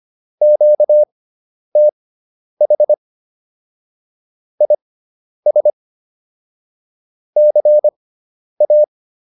Morse code, heard as a keyed tone, about 25 wpm, spaced as for 10 wpm, 600 Hz. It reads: QTH IS CA
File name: QSO102 10